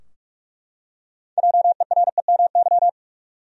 Morse code, heard as a keyed tone, about 44 wpm, 700 Hz. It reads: JEREMY
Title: USA Common Names - 45wpm